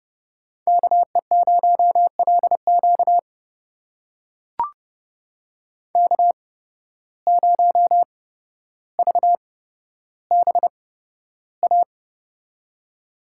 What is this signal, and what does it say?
Morse code, heard as a keyed tone, about 30 wpm, spaced as for 13 wpm, 700 Hz. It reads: KE0LQ K0VBA
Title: USA Call Signs - 5 Characters - 8x Character Spacing - 30wpm